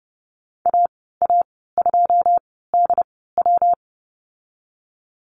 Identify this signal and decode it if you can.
Morse code, heard as a keyed tone, about 30 wpm, spaced as for 17 wpm, 700 Hz. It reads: AA2DW
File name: Lesson 19g - Letter w - Call Signs 30wpm